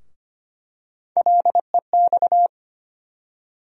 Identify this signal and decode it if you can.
Morse code, heard as a keyed tone, about 25 wpm, 700 Hz. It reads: LEX
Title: USA Call Sign Postfixes - 3 Letters 25wpm